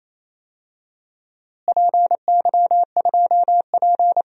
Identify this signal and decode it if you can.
Morse code, heard as a keyed tone, about 28 wpm, 700 Hz. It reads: PY2P
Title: DXCC-100 Call Signs - 4 Characters - Repeated 3x - 28wpm